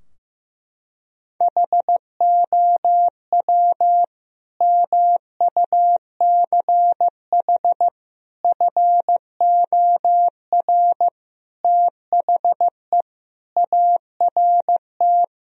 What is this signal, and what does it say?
Morse code, heard as a keyed tone, about 15 wpm, 700 Hz. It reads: HOW MUCH FOR THE ART
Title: Sentences from Top 1000 Words - Review Part 3 - 15wpm